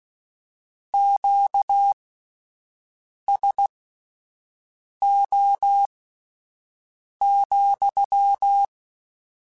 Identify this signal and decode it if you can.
Morse code, heard as a keyed tone, about 16 wpm, spaced as for 5 wpm, 800 Hz. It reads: QSO,
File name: Q Calls / Q Calls-mp3s / Q Calls sppc @10